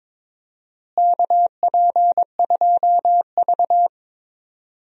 Morse code, heard as a keyed tone, about 21 wpm, 700 Hz. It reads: KP2V
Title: DXCC-100 Call Signs - 4 Characters - Repeated 3x - 22wpm